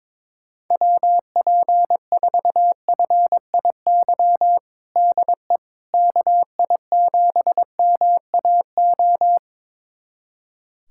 Morse code, heard as a keyed tone, about 22 wpm, 700 Hz. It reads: WP4FIY DE KI7MAO
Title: USA Callsign DE Callsign 22wpm